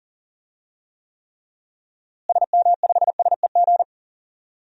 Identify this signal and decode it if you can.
Morse code, heard as a keyed tone, about 40 wpm, 700 Hz. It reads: SM5SEG
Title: DXCC-100 Call Signs - 6 Characters - Repeated 3x - 40wpm